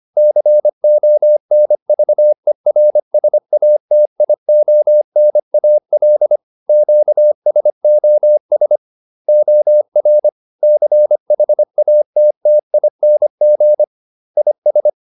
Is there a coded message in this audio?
Yes: Morse code from 0.2 to 14.9 s, heard as a keyed tone, about 25 wpm, 600 Hz. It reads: CONVERSATIONAL QSOS OR CHATTING IS